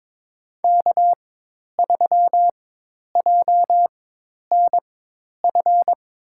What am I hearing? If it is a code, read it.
Morse code, heard as a keyed tone, about 22 wpm, spaced as for 10 wpm, 700 Hz. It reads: K3JNF
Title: USA Call Signs - 4x Character Spacing - 22wpm